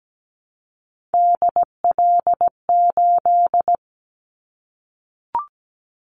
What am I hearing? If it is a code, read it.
Morse code, heard as a keyed tone, about 17 wpm, 700 Hz. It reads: DL8
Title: Callsign Prefixes for Top 50 DXCC - 17wpm